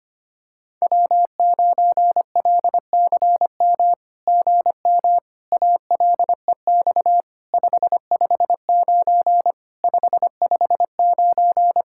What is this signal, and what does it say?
Morse code, heard as a keyed tone, about 24 wpm, 700 Hz. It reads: W9LCM GM ALEX 559 559
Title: SOTA QSO Practice 25wpm